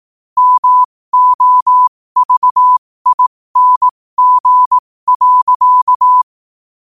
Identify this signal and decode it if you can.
Morse code, heard as a keyed tone, about 18 wpm, spaced as for 15 wpm, 1000 Hz. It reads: MOVING.
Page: 15 WPM morse code quotes